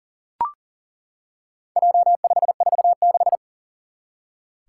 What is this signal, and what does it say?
Morse code, heard as a keyed tone, about 40 wpm, 700 Hz. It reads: J546
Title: Four Challenging Letter-Number Combinations 40wpm